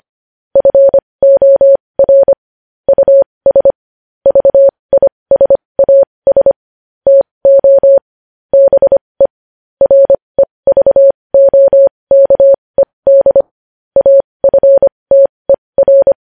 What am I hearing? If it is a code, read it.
Morse code, heard as a keyed tone, about 25 wpm, spaced as for 20 wpm, 550 Hz. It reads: FOR US VISAS TO BE REVOKED AFTER
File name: podcast / News headlines-1x-25x20